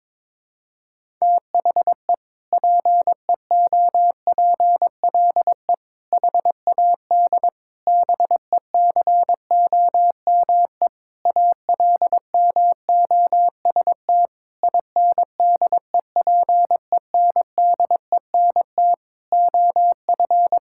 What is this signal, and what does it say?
Morse code, heard as a keyed tone, about 22 wpm, 700 Hz. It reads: THE PEOPLE HAD BECOME ALMOST INDEPENDENT OF
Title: Sentences from Top 1000 Words - Review Part 6 - 22wpm